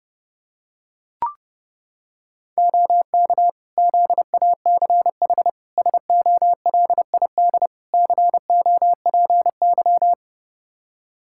Morse code, heard as a keyed tone, about 32 wpm, 700 Hz. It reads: OK ZACH SOLID COPY
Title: K1USN SST QSO Practice 30wpm